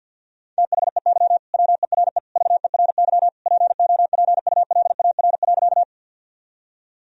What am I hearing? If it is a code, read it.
Morse code, heard as a keyed tone, about 50 wpm, 700 Hz. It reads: THEY WERE VERY POPULAR.
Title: Sentences from Top 1000 Words - Review Part 3 - 50wpm